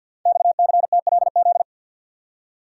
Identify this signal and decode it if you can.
Morse code, heard as a keyed tone, about 52 wpm, 700 Hz. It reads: XCTLZ
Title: Code Groups - 5 Letters - 50wpm